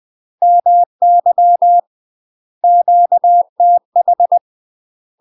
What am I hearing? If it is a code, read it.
Morse code, heard as a keyed tone, about 20 wpm, 700 Hz. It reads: MY QTH
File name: UK MASTER QSOs TRAINING / 20WPM / UK MASTER QSOs 0011